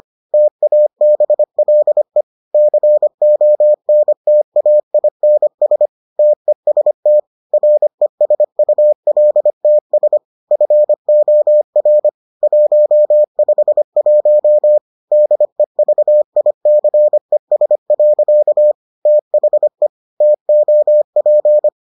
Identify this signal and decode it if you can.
Morse code, heard as a keyed tone, about 25 wpm, 600 Hz. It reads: TABLE CONTAINS TEST RESULTS FOR 151 DEVICES. THE TOP